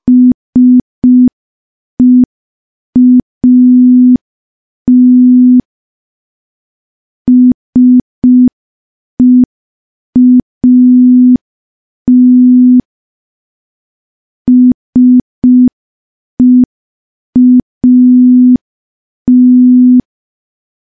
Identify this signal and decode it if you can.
Morse code, heard as a keyed tone, about 5 wpm, 250 Hz. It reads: SEAT SEAT SEAT
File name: morse / morse-training-05WPM